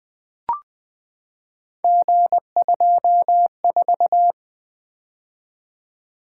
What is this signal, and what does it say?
Morse code, heard as a keyed tone, about 20 wpm, 700 Hz. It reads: G24
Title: Three Letter-Number Combinations - 20wpm